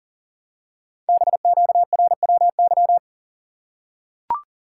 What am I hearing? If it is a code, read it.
Morse code, heard as a keyed tone, about 40 wpm, 700 Hz. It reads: BQRWY
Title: Code Groups - 5 Letters - 40wpm